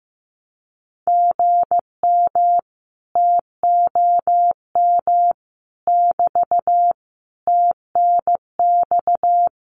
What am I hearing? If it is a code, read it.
Morse code, heard as a keyed tone, about 15 wpm, 700 Hz. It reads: GM TOM <BT> TNX
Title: Common QSO Phrases Encore 15wpm